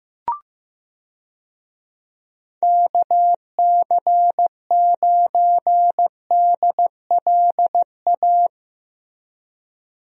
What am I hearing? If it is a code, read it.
Morse code, heard as a keyed tone, about 15 wpm, 700 Hz. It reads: KC9DLA
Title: USA Call Signs - 6 Characters - Repeated 3x - 15wpm